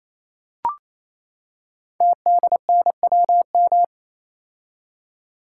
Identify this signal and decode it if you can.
Morse code, heard as a keyed tone, about 27 wpm, 700 Hz. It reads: TDNWM
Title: Code Groups - 5 Letters - 28wpm